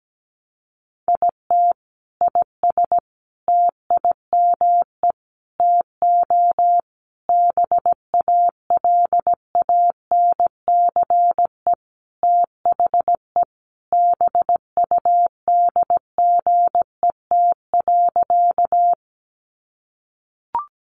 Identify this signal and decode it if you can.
Morse code, heard as a keyed tone, about 17 wpm, 700 Hz. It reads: IT IS TIME TO BALANCE THE BUDGET.
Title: Sentences from Top 1200 Words 17wpm